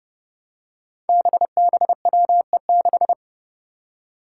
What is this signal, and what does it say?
Morse code, heard as a keyed tone, about 30 wpm, 700 Hz. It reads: BBWE6
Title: Five Challenging Letter-Number Combinations 30wpm